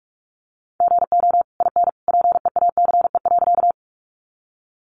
Morse code, heard as a keyed tone, about 44 wpm, 700 Hz. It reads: GO IN PEACE.
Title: Sentences from Top 1500 Words 45wpm